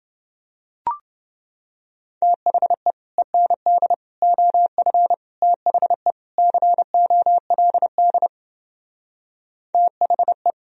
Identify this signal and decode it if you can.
Morse code, heard as a keyed tone, about 29 wpm, 700 Hz. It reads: THE END OF THE COLD THE